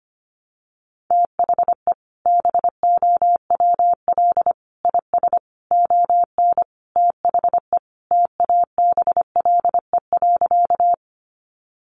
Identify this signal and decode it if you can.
Morse code, heard as a keyed tone, about 25 wpm, 700 Hz. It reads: THE BOWL IS ON THE TABLE.
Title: Sentences from Top 1200 Words 25wpm